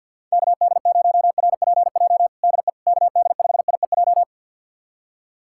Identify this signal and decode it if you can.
Morse code, heard as a keyed tone, about 51 wpm, 700 Hz. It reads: KD0RPJ DE KD5IEJ
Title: USA Callsign DE Callsign - 6 Characters - 50wpm